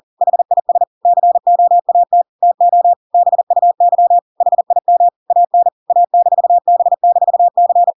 Morse code, heard as a keyed tone, about 40 wpm, 700 Hz. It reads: HIS COAT TO BUY HIM AN A-B-C